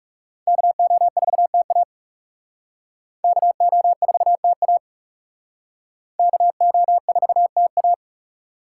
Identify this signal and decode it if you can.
Morse code, heard as a keyed tone, about 38 wpm, 700 Hz. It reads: KO4TA KO4TA KO4TA